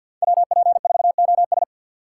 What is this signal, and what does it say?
Morse code, heard as a keyed tone, about 52 wpm, 700 Hz. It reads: WP4OS
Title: USA Call Signs - 5 Characters - 3x Character Spacing - 50wpm